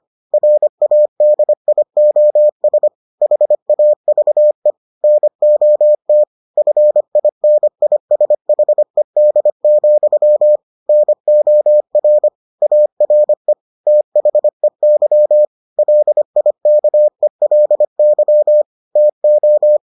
Morse code, heard as a keyed tone, about 25 wpm, 600 Hz. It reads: RADIOS HAVE NOT FINISHED, NOR ARE THEY LIKELY TO